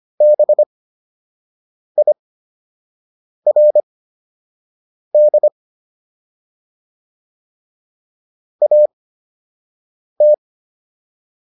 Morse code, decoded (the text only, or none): BIRD AT